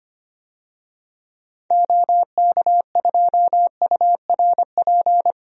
Morse code, heard as a keyed tone, about 25 wpm, 700 Hz. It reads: OK2URP
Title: DXCC-100 Call Signs - 6 Characters - 25wpm